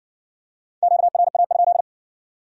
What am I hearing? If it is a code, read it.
Morse code, heard as a keyed tone, about 57 wpm, 700 Hz. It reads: QRA?